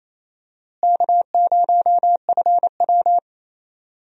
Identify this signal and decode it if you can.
Morse code, heard as a keyed tone, about 28 wpm, 700 Hz. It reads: K0FW
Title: USA Call Signs - 4 Characters - Repeated 2x - 28wpm